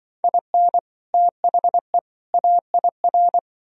Morse code, heard as a keyed tone, about 24 wpm, 700 Hz. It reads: IN THE AIR
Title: Speed-Racing - Sets of 3 Words 20wpm